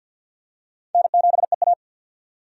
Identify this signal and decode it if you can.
Morse code, heard as a keyed tone, about 50 wpm, 700 Hz. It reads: N7EA
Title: USA Call Signs - 4 Characters - 50wpm